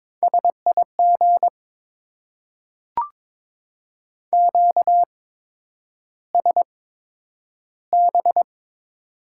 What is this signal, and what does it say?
Morse code, heard as a keyed tone, about 22 wpm, spaced as for 9 wpm, 700 Hz. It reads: SIG QSB
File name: Top 90 QSO Elements - 8x Character Spacing - 22wpm